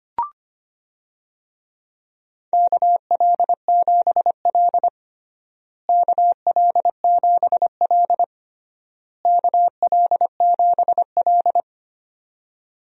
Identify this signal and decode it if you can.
Morse code, heard as a keyed tone, about 25 wpm, 700 Hz. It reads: KL7L KL7L KL7L
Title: DXCC-100 Call Signs - 4 Characters - Repeated 3x - 25wpm